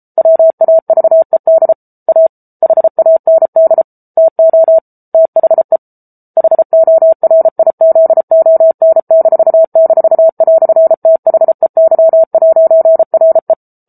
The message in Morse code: WAVED A HAND TO THE HORIZON--"THEY'RE

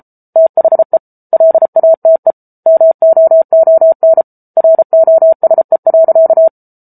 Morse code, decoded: THE LATE MOON ROSE.

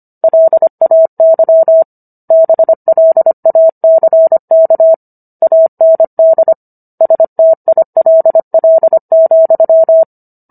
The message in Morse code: LAY BLACK AND STILL,

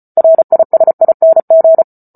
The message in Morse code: RISING

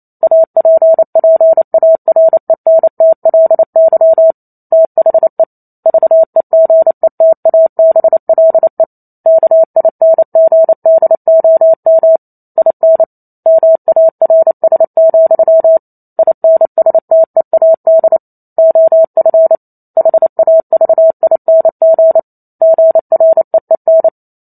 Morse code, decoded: APPARENTLY THE VEGETABLE KINGDOM IN MARS, INSTEAD OF HAVING GREEN

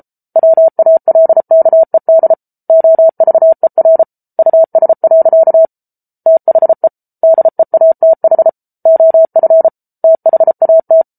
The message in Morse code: WALKED OVER US. THE DEATH OF THAT

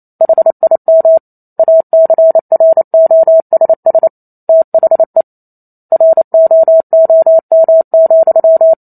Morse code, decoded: HIM ACROSS THE ROOM,

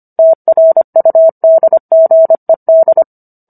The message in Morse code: TRUDGED

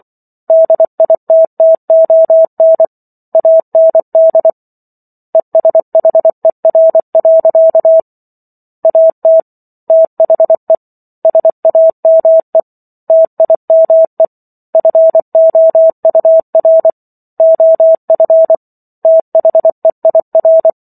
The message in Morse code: DITTON AND ESHER. AT THE SAME TIME FOUR OF THEIR

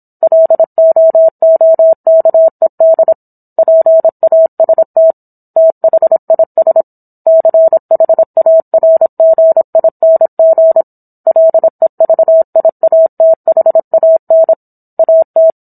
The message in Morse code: LOOKED PAST THIS CHARGING LEVIATHAN AT

